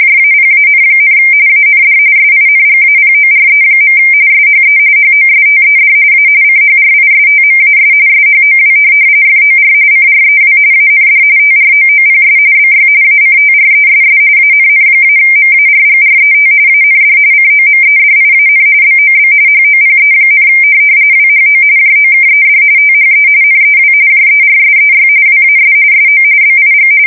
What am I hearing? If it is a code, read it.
RTTY, about 45.45 baud. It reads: JAPAN, AND SINGAPORE. SHE HOSTED A CHRISTMAS SHOW FOR CREW AND TROOPS BY ACTRESS ANN-MARGRET AND COMEDIAN BOB HOPE, WHO JOKED THE NEW JERSEY WAS SO BIG
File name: fsk170 / USS-New-Jersey